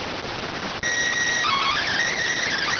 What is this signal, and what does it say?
SSTV (slow-scan television), PD 290.